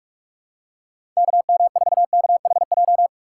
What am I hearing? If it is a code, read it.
Morse code, heard as a keyed tone, about 42 wpm, 700 Hz. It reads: KM4KHJ
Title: USA Call Signs - 6 Characters - 8x Character Spacing - 45wpm